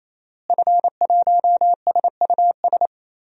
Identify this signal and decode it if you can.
Morse code, heard as a keyed tone, about 28 wpm, 700 Hz. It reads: F1SUS
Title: DXCC-100 Call Signs - 5 Characters - Repeated 3x - 28wpm